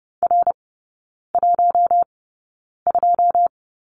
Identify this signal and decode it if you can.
Morse code, heard as a keyed tone, about 30 wpm, spaced as for 9 wpm, 700 Hz. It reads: R12
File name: 081 Lesson 14e - Number 2 - Three Characters 30wpm